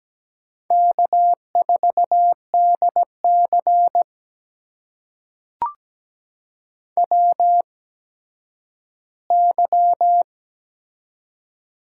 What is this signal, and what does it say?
Morse code, heard as a keyed tone, about 17 wpm, spaced as for 8 wpm, 700 Hz. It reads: K4DCW Y
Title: USA Call Signs - 8x Character Spacing - 17wpm